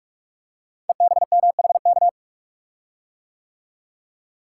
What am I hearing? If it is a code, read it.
Morse code, heard as a keyed tone, about 44 wpm, 700 Hz. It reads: EBMHK
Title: Code Groups - 5 Letters - 45wpm